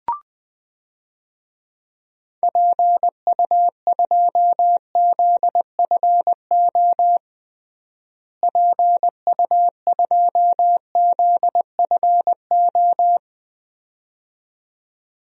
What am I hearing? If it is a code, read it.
Morse code, heard as a keyed tone, about 20 wpm, 700 Hz. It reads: PU2ZFO PU2ZFO
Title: DXCC-100 Call Signs - 6 Characters - Repeated 2x - 20wpm